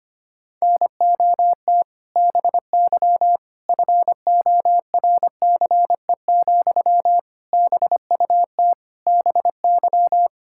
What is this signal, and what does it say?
Morse code, heard as a keyed tone, about 25 wpm, 700 Hz. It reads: NOT BY FORCE, BUT BY
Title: Sentences from Top 1000 Words - Review Part 5 - 25wpm